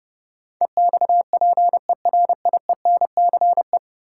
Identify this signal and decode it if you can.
Morse code, heard as a keyed tone, about 32 wpm, 700 Hz. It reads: EXPERIENCE
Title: Sentences from Top 600 Words - New Word Drill - 30wpm